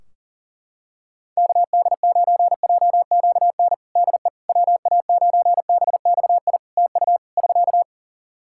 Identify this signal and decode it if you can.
Morse code, heard as a keyed tone, about 40 wpm, 700 Hz. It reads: KD9JQN DE WA9BXI TU <SK>
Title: Practice QSO - Exchange 4 - Ending - 40wpm